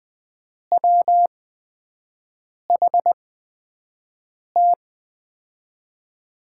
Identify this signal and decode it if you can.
Morse code, heard as a keyed tone, about 20 wpm, spaced as for 5 wpm, 700 Hz. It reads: WHT
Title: USA Call Signs - 8x Character Spacing - 20wpm